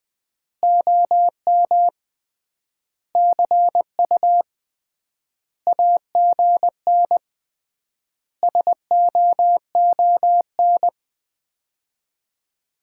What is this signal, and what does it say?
Morse code, heard as a keyed tone, about 20 wpm, 700 Hz. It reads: OM CU AGN SOON
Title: Common QSO Phrases - 3x Word-Spacing - 20wpm